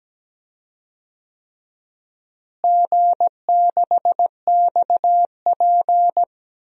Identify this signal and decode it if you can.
Morse code, heard as a keyed tone, about 17 wpm, 700 Hz. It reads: G6XP
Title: DXCC-100 Call Signs - 4 Characters - Repeated 3x - 17wpm